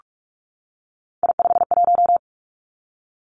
Morse code, heard as a keyed tone, about 46 wpm, 700 Hz. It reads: I51
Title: Three Challenging Letter-Number Combinations 45wpm